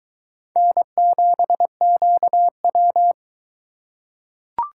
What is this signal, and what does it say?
Morse code, heard as a keyed tone, about 21 wpm, 700 Hz. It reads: N7QW